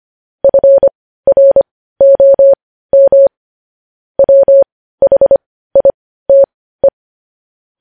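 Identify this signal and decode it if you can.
Morse code, heard as a keyed tone, about 25 wpm, spaced as for 15 wpm, 550 Hz. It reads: FROM WHITE
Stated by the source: Morse code practice of news headlines at 15 wpm.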